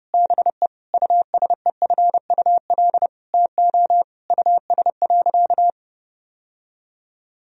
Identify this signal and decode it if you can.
Morse code, heard as a keyed tone, about 30 wpm, 700 Hz. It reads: BE USEFUL TO US.
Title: Sentences from Top 1000 Words - Review Part 4 - 30wpm